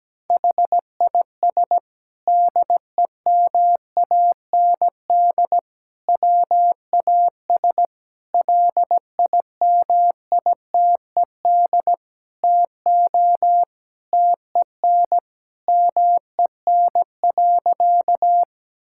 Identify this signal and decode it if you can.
Morse code, heard as a keyed tone, about 17 wpm, 700 Hz. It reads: HIS DEMAND WAS LIMITED TO TEN MEN.
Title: Sentences from Top 1000 Words - Review Part 6 - 17wpm